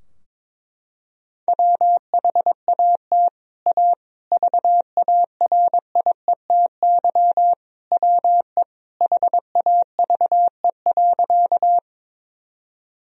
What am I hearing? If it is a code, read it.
Morse code, heard as a keyed tone, about 22 wpm, 700 Hz. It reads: WHAT A VARIETY WE HAVE.
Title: Sentences from Top 1000 Words - Review Part 4 - 22wpm